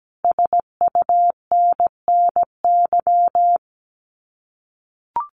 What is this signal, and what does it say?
Morse code, heard as a keyed tone, about 17 wpm, 700 Hz. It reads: SUNNY
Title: Common QSO Phrases Encore 17wpm